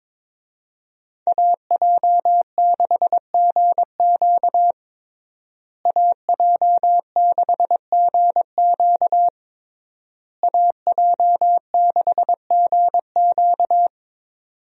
Morse code, heard as a keyed tone, about 22 wpm, 700 Hz. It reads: AJ6GQ AJ6GQ AJ6GQ
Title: USA Call Signs - 5 Characters - Repeated 3x - 22wpm